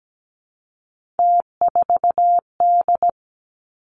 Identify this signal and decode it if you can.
Morse code, heard as a keyed tone, about 17 wpm, 700 Hz. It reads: T4D
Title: Three Challenging Letter-Number Combinations 17wpm